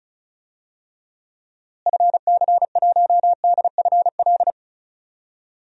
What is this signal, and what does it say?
Morse code, heard as a keyed tone, about 34 wpm, 700 Hz. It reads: FC1DFL
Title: DXCC-100 Call Signs - 6 Characters - Repeated 2x - 35wpm